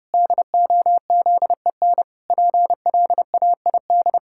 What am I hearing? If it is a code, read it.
Morse code, heard as a keyed tone, about 30 wpm, 700 Hz. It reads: DOZEN PLAID
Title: Pangram Challenge v2 - 30wpm